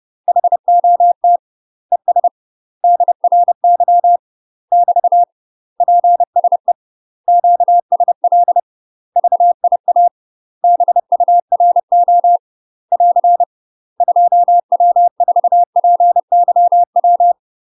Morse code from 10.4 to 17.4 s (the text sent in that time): BURO <AR> 2W4PYW